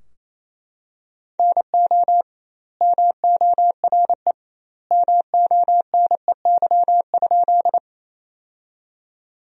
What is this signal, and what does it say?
Morse code, heard as a keyed tone, about 28 wpm, 700 Hz. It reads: NO MORE MONEY?
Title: 2x Word-Spacing - Sentences from Top 400 Words - 28wpm